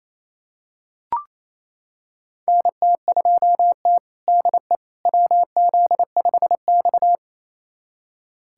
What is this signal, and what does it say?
Morse code, heard as a keyed tone, about 28 wpm, 700 Hz. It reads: NT2T DE WZ5X
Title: USA Callsign DE Callsign - 4 Characters - 28wpm